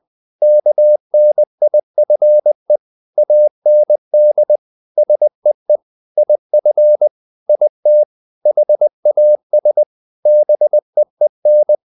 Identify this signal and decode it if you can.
Morse code, heard as a keyed tone, about 20 wpm, 600 Hz. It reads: KNIFE AND SEE IF IT HAS BEEN
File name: the old man and the sea part 2 20wpm